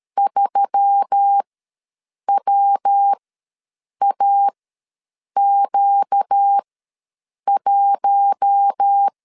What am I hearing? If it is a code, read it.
Morse code, heard as a keyed tone, about 13 wpm, spaced as for 7 wpm, 800 Hz. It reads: 3WAQ1
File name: LEZIONI / 8 YJ45 / 8 YJ4 ws 12wpm 3249